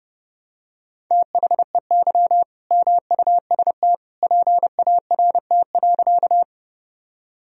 Morse code, heard as a keyed tone, about 30 wpm, 700 Hz. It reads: THEY MUST PART.
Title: Sentences from Top 1000 Words - Review Part 1 - 30wpm